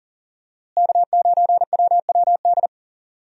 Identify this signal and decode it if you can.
Morse code, heard as a keyed tone, about 40 wpm, 700 Hz. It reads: K9WWD
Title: USA Call Signs - 5 Characters - Repeated 3x - 40wpm